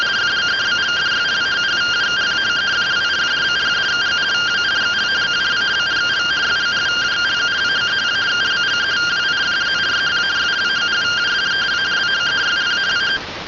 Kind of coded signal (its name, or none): two-tone teleprinter signal